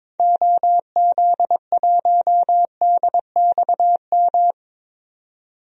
Morse code, heard as a keyed tone, about 22 wpm, 700 Hz. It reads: OZ1DXM
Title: DXCC-100 Call Signs - 6 Characters - 22wpm